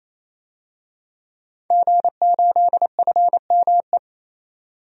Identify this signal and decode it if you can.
Morse code, heard as a keyed tone, about 28 wpm, 700 Hz. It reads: G8FME
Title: DXCC-100 Call Signs - 5 Characters - 28wpm